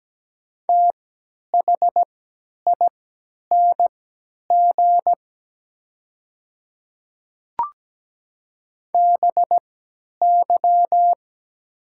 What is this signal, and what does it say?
Morse code, heard as a keyed tone, about 17 wpm, spaced as for 10 wpm, 700 Hz. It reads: THING BY
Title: 3x Character Spacing - Top 100 Words - 17wpm